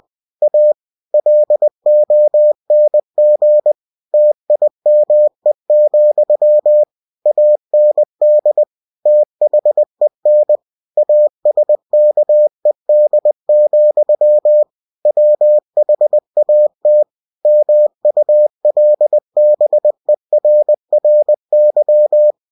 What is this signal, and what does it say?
Morse code, heard as a keyed tone, about 20 wpm, 600 Hz. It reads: A LONG TIME, AND THEN ASKED, WHAT MULBERRY
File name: cw-books / winnie-the-pooh / winnie-20wpm-page-0045